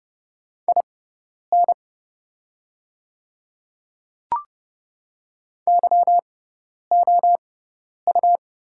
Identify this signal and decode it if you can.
Morse code, heard as a keyed tone, about 30 wpm, spaced as for 10 wpm, 700 Hz. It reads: IN YOU
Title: 6x Character Spacing - Top 100 Words - 30wpm